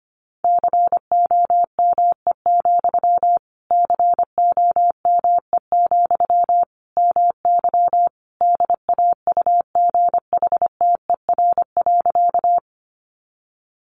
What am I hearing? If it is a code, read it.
Morse code, heard as a keyed tone, about 25 wpm, 700 Hz. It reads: COME, COME, MY DAUGHTER.
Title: Sentences from Top 1500 Words 25wpm